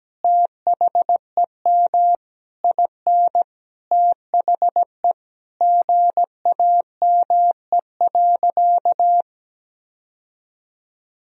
Morse code, heard as a keyed tone, about 17 wpm, 700 Hz. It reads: THEM IN THE GAME.
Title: Sentences from Top 1000 Words - Review Part 3 - 17wpm